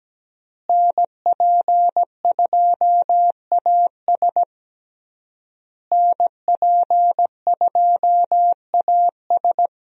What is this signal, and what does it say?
Morse code, heard as a keyed tone, about 17 wpm, 700 Hz. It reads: NP2AS NP2AS
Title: USA Call Signs - 5 Characters - Repeated 3x - 17wpm